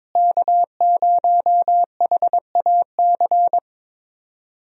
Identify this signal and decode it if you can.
Morse code, heard as a keyed tone, about 22 wpm, 700 Hz. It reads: K0HAC
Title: USA Call Signs - 5 Characters - Repeated 3x - 22wpm